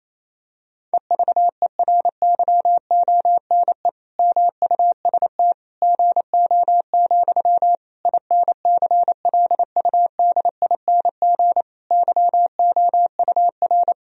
Morse code, heard as a keyed tone, about 28 wpm, 700 Hz. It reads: EVERYONE MUST GO, INCLUDING YOUR
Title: Sentences from Top 1000 Words - Review Part 4 - 28wpm